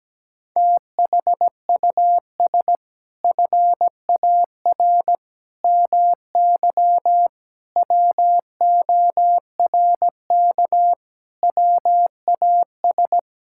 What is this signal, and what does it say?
Morse code, heard as a keyed tone, about 17 wpm, 700 Hz. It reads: THUS FAR MY WORK WAS
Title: Sentences from Top 1000 Words - Review Part 5 - 17wpm